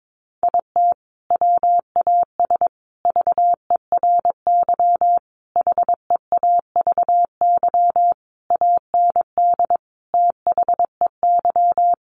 Morse code, heard as a keyed tone, about 22 wpm, 700 Hz. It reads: IT WAS VERY HEAVY AND THEY
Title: Sentences from Top 1100 Words 22wpm